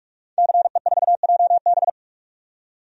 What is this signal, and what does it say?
Morse code, heard as a keyed tone, about 44 wpm, 690 Hz. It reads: CE4JB
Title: DXCC-100 Call Signs - 5 Characters - 45wpm